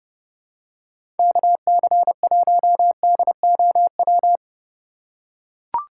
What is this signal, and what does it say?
Morse code, heard as a keyed tone, about 30 wpm, 700 Hz. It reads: KC1DOW
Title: USA Call Signs - 6 Characters - 4x Character Spacing - 30wpm